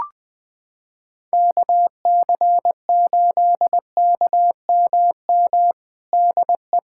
Morse code, heard as a keyed tone, about 20 wpm, 700 Hz. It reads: KC8KMM DE
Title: USA Callsign DE Callsign 20wpm